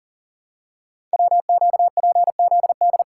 Morse code, heard as a keyed tone, about 38 wpm, 690 Hz. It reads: WQPZD
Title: Code Groups - 5 Letter:Numbers - 40wpm